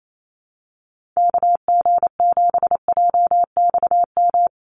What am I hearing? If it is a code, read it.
Morse code, heard as a keyed tone, about 28 wpm, 700 Hz. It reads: KG7JXM
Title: USA Call Signs 28wpm